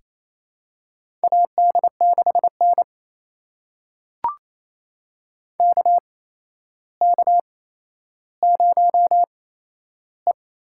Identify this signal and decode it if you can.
Morse code, heard as a keyed tone, about 28 wpm, spaced as for 12 wpm, 700 Hz. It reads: AD6N KK0E
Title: USA Call Signs - 4 Characters - 8x Character Spacing - 28wpm